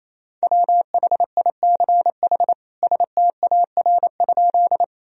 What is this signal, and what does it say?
Morse code, heard as a keyed tone, about 28 wpm, 700 Hz. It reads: WHICH STAR?
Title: Sentences from Top 1000 Words - Review Part 2 - 28wpm